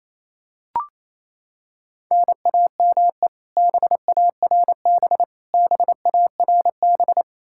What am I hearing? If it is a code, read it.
Morse code, heard as a keyed tone, about 28 wpm, 700 Hz. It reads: NAME BARB BARB
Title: Practice QSO - Exchange 1 - Beginning - 28wpm